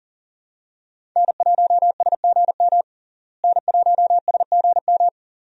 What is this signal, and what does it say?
Morse code, heard as a keyed tone, about 37 wpm, 700 Hz. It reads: N1SGM N1SGM
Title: USA Call Signs - 5 Characters - Repeated 2x - 40wpm